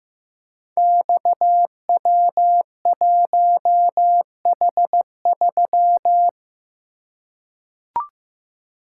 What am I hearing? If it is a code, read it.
Morse code, heard as a keyed tone, about 15 wpm, 700 Hz. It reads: XW1H3
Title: Code Groups - 5 Letter:Numbers - 15wpm